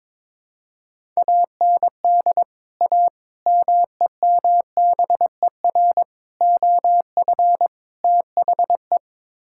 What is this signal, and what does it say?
Morse code, heard as a keyed tone, about 22 wpm, 700 Hz. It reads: AND A MEMBER OF THE
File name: practice / 22wpm / Build A Phrase - 5-Word Phrases - 22wpm